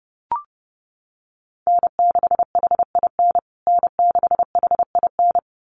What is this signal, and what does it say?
Morse code, heard as a keyed tone, about 30 wpm, 700 Hz. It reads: N6HIN N6HIN
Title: USA Call Signs Repeated 2x 30wpm